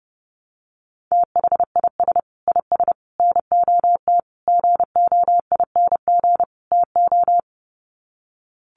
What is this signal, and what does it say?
Morse code, heard as a keyed tone, about 30 wpm, 700 Hz. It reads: THIS IS NOT GOING TO
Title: Sets of 5 Words Encore2 30wpm